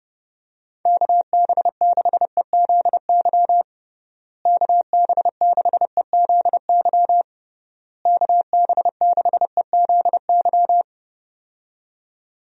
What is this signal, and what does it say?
Morse code, heard as a keyed tone, about 30 wpm, 700 Hz. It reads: KB6EZY KB6EZY KB6EZY
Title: USA Call Signs - 6 Characters - Repeated 3x - 30wpm